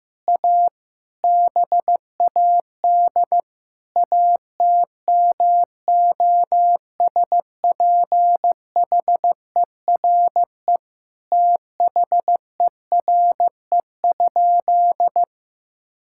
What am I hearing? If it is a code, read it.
Morse code, heard as a keyed tone, about 15 wpm, 700 Hz. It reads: A BAD ATMOSPHERE THERE?
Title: Sentences from Top 1600 Words 15wpm